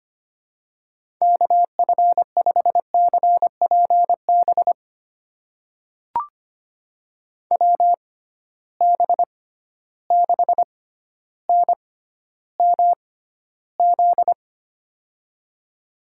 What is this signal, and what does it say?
Morse code, heard as a keyed tone, about 25 wpm, spaced as for 13 wpm, 700 Hz. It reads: KF5CPB WB6NMZ
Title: USA Call Signs - 6 Characters - 6x Character Spacing - 25wpm